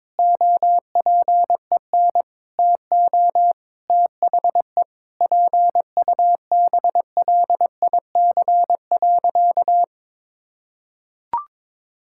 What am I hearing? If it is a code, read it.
Morse code, heard as a keyed tone, about 22 wpm, 700 Hz. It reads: OPEN TO THE PUBLIC.
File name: Sentences with 7-Word Phrases - 22wpm